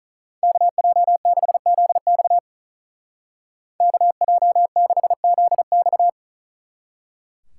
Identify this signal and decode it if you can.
Morse code, heard as a keyed tone, about 43 wpm, 700 Hz. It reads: KJ6ZX KJ6ZX
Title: Speed Racing - USA 2x2 Call Signs 35wpm